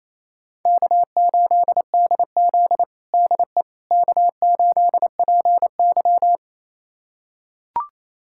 Morse code, heard as a keyed tone, about 28 wpm, 700 Hz. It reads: K8DZ DE K8PY
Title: USA Callsign DE Callsign - 4 Characters - 28wpm